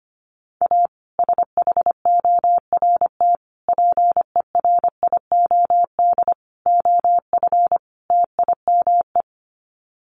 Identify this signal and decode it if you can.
Morse code, heard as a keyed tone, about 25 wpm, 700 Hz. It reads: A SHORT PERIOD OF TIME
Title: Sets of 5 Words Encore 25wpm